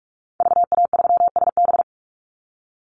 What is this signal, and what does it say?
Morse code, heard as a keyed tone, about 44 wpm, 700 Hz. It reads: VA3SB
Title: Call Signs Repeated 2x - 45wpm